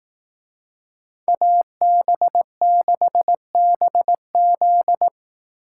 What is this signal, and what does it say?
Morse code, heard as a keyed tone, about 18 wpm, 700 Hz. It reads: AB6BZ
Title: Speed Racing - USA 2x2 Call Signs 15wpm